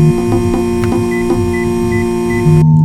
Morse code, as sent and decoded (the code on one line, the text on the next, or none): none